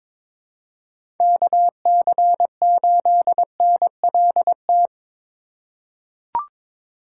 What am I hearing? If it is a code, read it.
Morse code, heard as a keyed tone, about 22 wpm, 700 Hz. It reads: KC8NLT
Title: USA Call Signs - 8x Character Spacing - 22wpm